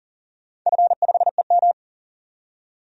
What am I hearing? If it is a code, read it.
Morse code, heard as a keyed tone, about 40 wpm, 700 Hz. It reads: F5EM
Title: DXCC-100 Call Signs - 4 Characters - 40wpm